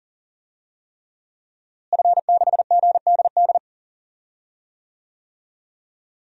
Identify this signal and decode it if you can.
Morse code, heard as a keyed tone, about 39 wpm, 700 Hz. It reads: F6GDD
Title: DXCC-100 Call Signs - 5 Characters - Repeated 3x - 40wpm